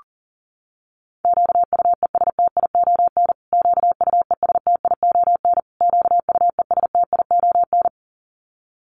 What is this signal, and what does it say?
Morse code, heard as a keyed tone, about 40 wpm, 700 Hz. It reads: QUESTION QUESTION QUESTION
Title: Top 500 Words Repeated 40wpm